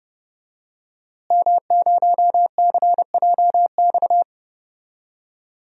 Morse code, heard as a keyed tone, about 30 wpm, 700 Hz. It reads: M0CJX
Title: DXCC-100 Call Signs - 5 Characters - Repeated 3x - 30wpm